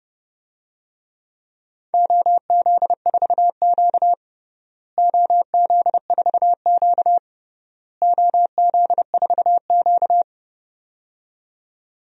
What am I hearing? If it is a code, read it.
Morse code, heard as a keyed tone, about 30 wpm, 700 Hz. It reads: OZ4Q OZ4Q OZ4Q
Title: DXCC-100 Call Signs - 4 Characters - Repeated 3x - 30wpm